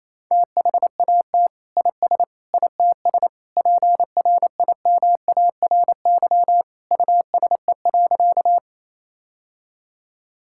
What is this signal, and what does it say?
Morse code, heard as a keyed tone, about 28 wpm, 700 Hz. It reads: THAT IS ITS PRIMARY USE.
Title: Sentences from Top 1000 Words - Review Part 5 - 28wpm